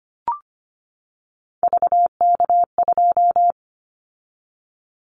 Morse code, heard as a keyed tone, about 25 wpm, 700 Hz. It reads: VK2
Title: Callsign Prefixes for Top 100 DXCC - 25wpm